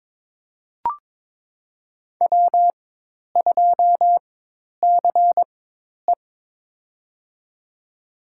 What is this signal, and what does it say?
Morse code, heard as a keyed tone, about 22 wpm, spaced as for 10 wpm, 700 Hz. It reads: W2CE
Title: USA Call Signs - 4 Characters - 4x Character Spacing - 22wpm